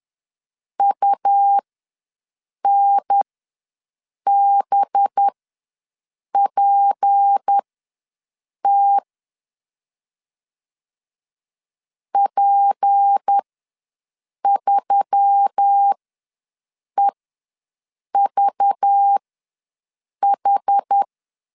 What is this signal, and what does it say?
Morse code, heard as a keyed tone, about 10 wpm, spaced as for 6 wpm, 800 Hz. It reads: UNBPT P3EVH
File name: LEZIONI / 6 PKX0 / 6 P ws 10wpm 3751